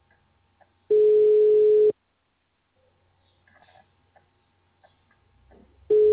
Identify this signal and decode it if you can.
Ringback tone (Europe, 425 Hz).
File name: out- -2047- - - .938